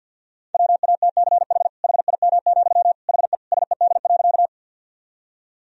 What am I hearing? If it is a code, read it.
Morse code, heard as a keyed tone, about 50 wpm, 700 Hz. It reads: WATCH HIM, HE SED.
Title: Sentences from Top 300 Words and Abbreviations 50wpm